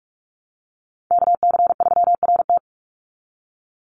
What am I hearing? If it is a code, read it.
Morse code, heard as a keyed tone, about 44 wpm, 700 Hz. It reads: KC3RT
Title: NA SOTA Chaser Callsigns 45wpm